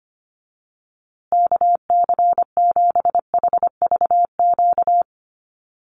Morse code, heard as a keyed tone, about 25 wpm, 700 Hz. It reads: KC7HVQ
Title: USA Call Signs Repeated 3x 25wpm